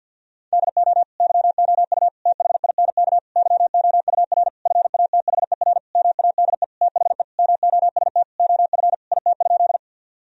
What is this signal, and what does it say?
Morse code, heard as a keyed tone, about 51 wpm, 700 Hz. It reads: DO YOU THINK YOUR FATHER MADE THE MOST OF IT?
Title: Sentences from Top 1000 Words - Review Part 1 - 50wpm